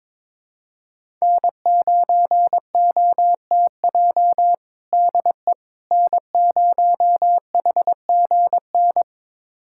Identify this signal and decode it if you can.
Morse code, heard as a keyed tone, about 22 wpm, 700 Hz. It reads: N9OTJ DE N0HGN
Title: USA Callsign DE Callsign - 5 Characters - 22wpm